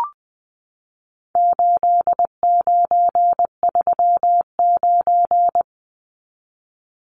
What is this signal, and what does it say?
Morse code, heard as a keyed tone, about 20 wpm, 700 Hz. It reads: 8939
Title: Numbers - 4-Digit 20wpm